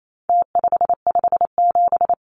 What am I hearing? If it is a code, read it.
Morse code, heard as a keyed tone, about 28 wpm, 700 Hz. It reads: T557
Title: Numbers - Contest Style - 4-Digit 28wpm